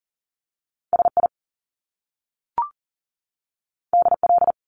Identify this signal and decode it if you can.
Morse code, heard as a keyed tone, about 40 wpm, 700 Hz. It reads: SI DL
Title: Top Two Letter Combinations in Words 40wpm